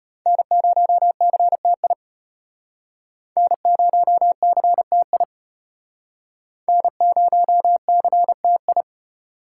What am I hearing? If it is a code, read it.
Morse code, heard as a keyed tone, about 34 wpm, 700 Hz. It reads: N0CTI N0CTI N0CTI